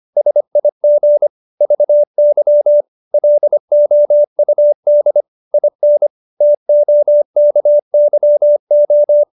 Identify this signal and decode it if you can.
Morse code, heard as a keyed tone, about 25 wpm, 590 Hz. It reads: SIG VY LOUD IN TOKYO
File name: QSO305 25